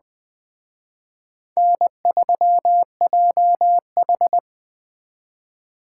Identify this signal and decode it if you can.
Morse code, heard as a keyed tone, about 20 wpm, 700 Hz. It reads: N3JH
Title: Speed Racing - USA Call Signs 17wpm